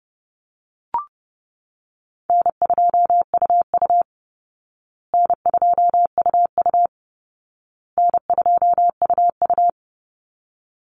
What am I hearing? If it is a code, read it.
Morse code, heard as a keyed tone, about 30 wpm, 700 Hz. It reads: N2UU N2UU N2UU
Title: Easy Call Signs Repeated 3x - 30wpm